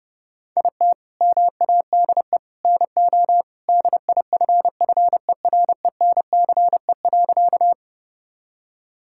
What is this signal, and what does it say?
Morse code, heard as a keyed tone, about 32 wpm, 700 Hz. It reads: IT MADE NO DIFFERENCE.
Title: Sentences from Top 1000 Words - Review Part 5 - 30wpm